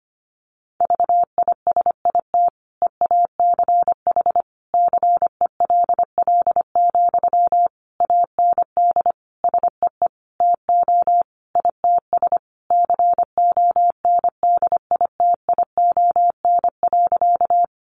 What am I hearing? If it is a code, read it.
Morse code, heard as a keyed tone, about 25 wpm, 700 Hz. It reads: VISIT EACH CELL, AND SEE TO ITS CONDITION.
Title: Sentences from Top 1100 Words 25wpm